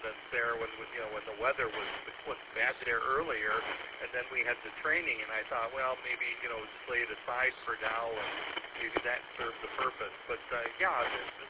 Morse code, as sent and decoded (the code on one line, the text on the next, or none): none